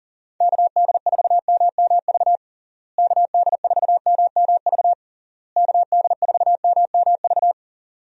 Morse code, heard as a keyed tone, about 40 wpm, 700 Hz. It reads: KD4MMV KD4MMV KD4MMV
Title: USA Call Signs - 6 Characters - Repeated 3x - 40wpm